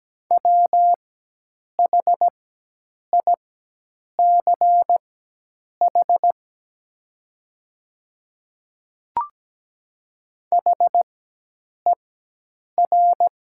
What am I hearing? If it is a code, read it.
Morse code, heard as a keyed tone, about 17 wpm, spaced as for 8 wpm, 700 Hz. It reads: WHICH HER
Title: 4x Character Spacing - Top 200 Words - 17wpm